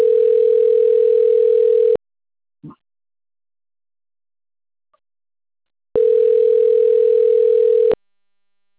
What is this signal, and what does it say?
Ringback tone (North America), 440 + 480 Hz.
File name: rg-1600-unknown- - - .87626